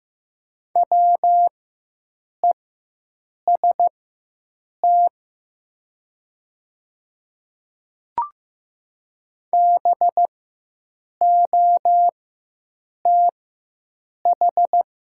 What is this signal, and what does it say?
Morse code, heard as a keyed tone, about 15 wpm, spaced as for 7 wpm, 700 Hz. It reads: WEST BOTH
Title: 4x Character Spacing - Top 500 Words - 15wpm